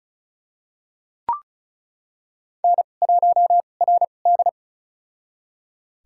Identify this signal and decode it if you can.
Morse code, heard as a keyed tone, about 34 wpm, spaced as for 25 wpm, 700 Hz. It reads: N1RD
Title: USA Call Signs - 4 Characters - 2x Character Spacing - 35wpm